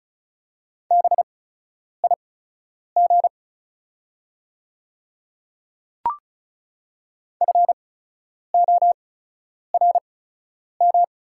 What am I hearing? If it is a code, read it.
Morse code, heard as a keyed tone, about 33 wpm, spaced as for 9 wpm, 690 Hz. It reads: BIG FORM
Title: 8x Character Spacing - Top 200 Words - 35wpm